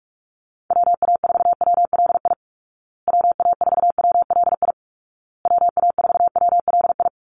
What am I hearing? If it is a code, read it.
Morse code, heard as a keyed tone, about 44 wpm, 700 Hz. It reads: WA4WLI WA4WLI WA4WLI
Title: Call Signs Repeated 2x - 45wpm